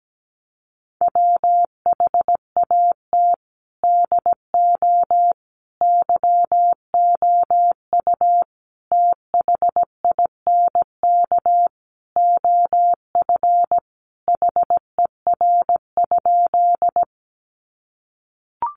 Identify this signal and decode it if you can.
Morse code, heard as a keyed tone, about 17 wpm, 700 Hz. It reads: WHAT DO YOU THINK OF HER?
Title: Sentences from Top 200 Words 17wpm